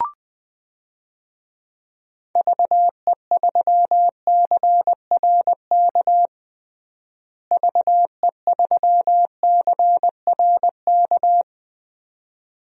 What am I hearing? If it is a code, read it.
Morse code, heard as a keyed tone, about 20 wpm, 700 Hz. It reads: VE3CRK VE3CRK
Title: DXCC-100 Call Signs - 6 Characters - Repeated 3x - 20wpm